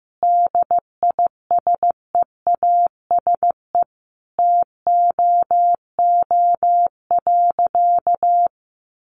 Morse code, heard as a keyed tone, about 14 wpm, 700 Hz. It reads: DISEASE TOO.
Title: Sentences from Top 1500 Words 15wpm